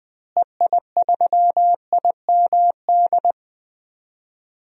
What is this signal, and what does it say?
Morse code, heard as a keyed tone, about 20 wpm, 690 Hz. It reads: EI3IMD
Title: DXCC-100 Call Signs - 6 Characters - Repeated 2x - 20wpm